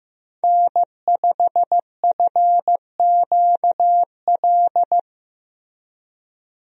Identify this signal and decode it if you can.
Morse code, heard as a keyed tone, about 15 wpm, 700 Hz. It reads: N5FQL
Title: USA Call Signs - 5 Characters - Repeated 2x - 15wpm